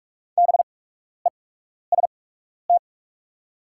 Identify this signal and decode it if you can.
Morse code, heard as a keyed tone, about 44 wpm, spaced as for 12 wpm, 700 Hz. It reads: BEST
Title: 8x Character Spacing - Top 500 Words - 45wpm